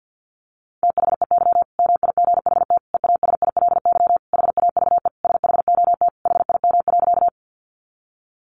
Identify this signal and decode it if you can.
Morse code, heard as a keyed tone, about 50 wpm, 700 Hz. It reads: THEY MIGHT EASILY HAVE SHOT HIM.
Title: Sentences from Top 1100 Words 50wpm